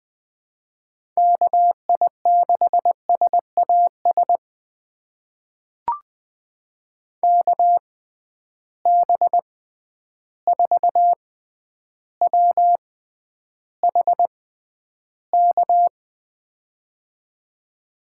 Morse code, heard as a keyed tone, about 20 wpm, spaced as for 10 wpm, 700 Hz. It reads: KI6SAS KB4WHK
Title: USA Call Signs - 6 Characters - 6x Character Spacing - 20wpm